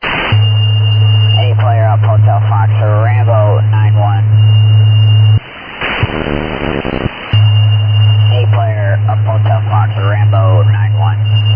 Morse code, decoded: none

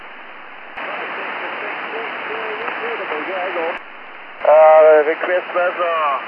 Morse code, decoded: none

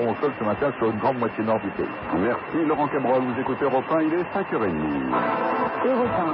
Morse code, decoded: none

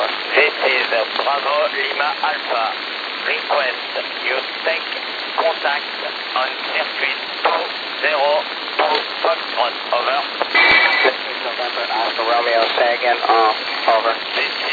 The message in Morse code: none

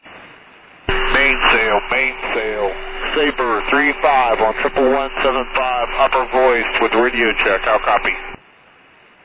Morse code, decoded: none

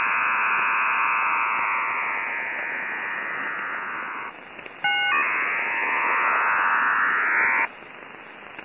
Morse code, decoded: none

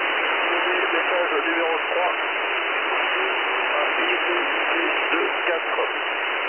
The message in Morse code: none

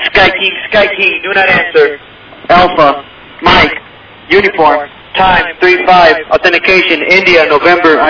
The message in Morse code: none

